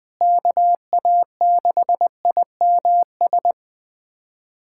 Morse code, heard as a keyed tone, about 20 wpm, 700 Hz. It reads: KA6IMS
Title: USA Callsign DE Callsign - 6 Characters - 20wpm